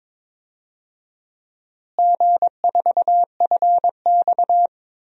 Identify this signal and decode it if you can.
Morse code, heard as a keyed tone, about 21 wpm, 700 Hz. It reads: G4FX
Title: DXCC-100 Call Signs - 4 Characters - Repeated 2x - 22wpm